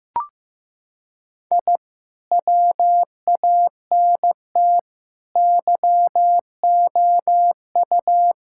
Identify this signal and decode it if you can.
Morse code, heard as a keyed tone, about 15 wpm, 700 Hz. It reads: I WANT YOU
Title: Sentences from Top 1000 Words - Review Part 6 - 15wpm